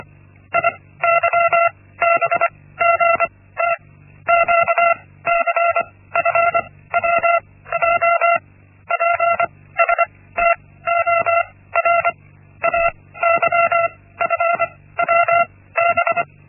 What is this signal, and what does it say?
Morse code, heard as a keyed tone, about 24 wpm, spaced as for 16 wpm, 680 Hz. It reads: IYBGT QCFWJ PSTOR AYFWB
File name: Linksmakalnis-1